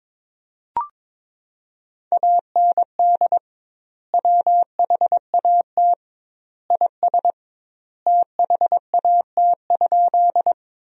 Morse code, heard as a keyed tone, about 22 wpm, 700 Hz. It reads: AND WHAT IS THAT?
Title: 2x Word-Spacing - Sentences from Top 100 Words - 22wpm